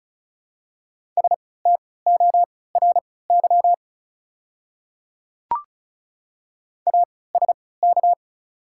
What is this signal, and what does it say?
Morse code, heard as a keyed tone, about 36 wpm, spaced as for 20 wpm, 690 Hz. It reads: STORY ASK
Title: 3x Character Spacing - Top 300 Words - 35wpm